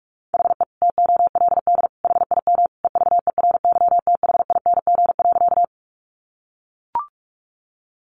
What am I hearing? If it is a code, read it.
Morse code, heard as a keyed tone, about 44 wpm, 700 Hz. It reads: HE TOLD HIM EVERYTHING.
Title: Sentences from Top 700 Words 45wpm